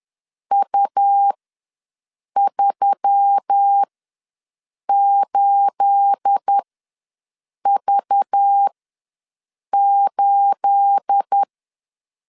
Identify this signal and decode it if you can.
Morse code, heard as a keyed tone, about 10 wpm, spaced as for 6 wpm, 800 Hz. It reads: U38V8
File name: LEZIONI / 9 28 ws 10wpm 4124